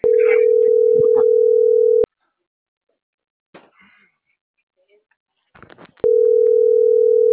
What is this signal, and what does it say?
Ringback tone (North America), 440 + 480 Hz.